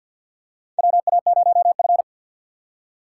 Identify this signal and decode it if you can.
Morse code, heard as a keyed tone, about 54 wpm, 700 Hz. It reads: WA0F
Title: USA Call Signs - 6x Character Spacing - 50wpm